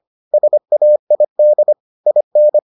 Morse code, heard as a keyed tone, about 25 wpm, 600 Hz. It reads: SAID IN